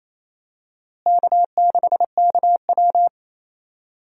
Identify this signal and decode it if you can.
Morse code, heard as a keyed tone, about 28 wpm, 700 Hz. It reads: K6KW
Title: USA Call Signs - 4 Characters - Repeated 2x - 28wpm